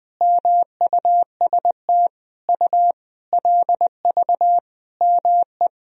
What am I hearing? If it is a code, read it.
Morse code, heard as a keyed tone, about 21 wpm, 700 Hz. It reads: MUST U LV ME
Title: Sentences from Top 400 Words and Abbreviations 20wpm